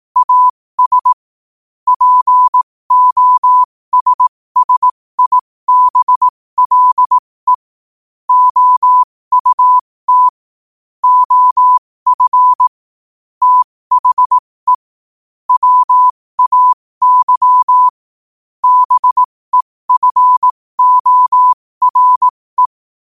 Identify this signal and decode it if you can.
Morse code, heard as a keyed tone, about 18 wpm, spaced as for 15 wpm, 1000 Hz. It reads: AS POSSIBLE OUT OF THE WAY BEFORE
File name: qotdfiles / 15wpm